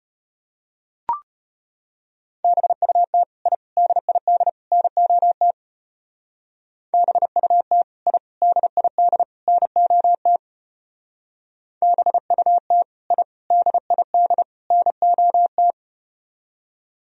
Morse code, heard as a keyed tone, about 33 wpm, 700 Hz. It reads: BUT I DID NOT BUT I DID NOT BUT I DID NOT